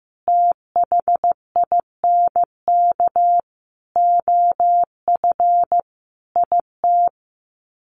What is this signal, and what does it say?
Morse code, heard as a keyed tone, about 15 wpm, 700 Hz. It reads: THINK OF IT
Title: Sets of 5 Words Encore 15wpm